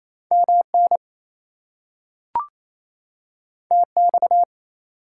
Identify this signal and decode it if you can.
Morse code, heard as a keyed tone, about 28 wpm, 700 Hz. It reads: MN TX
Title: US States and Canadian Provinces 28wpm